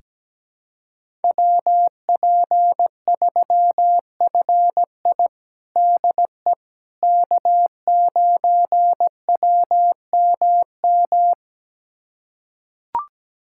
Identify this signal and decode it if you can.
Morse code, heard as a keyed tone, about 17 wpm, 700 Hz. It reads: WP3FI DE K9WMM
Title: USA Callsign DE Callsign - 5 Characters - 17wpm